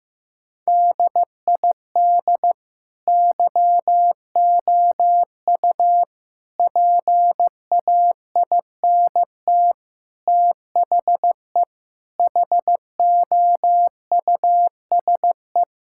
Morse code, heard as a keyed tone, about 16 wpm, 700 Hz. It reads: DID YOU PAINT THE HOUSE
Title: Sentences from Top 1000 Words - Review Part 6 - 15wpm